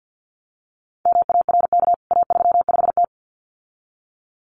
Morse code, heard as a keyed tone, about 50 wpm, 700 Hz. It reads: MARK A35T
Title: CWT Exchange 50wpm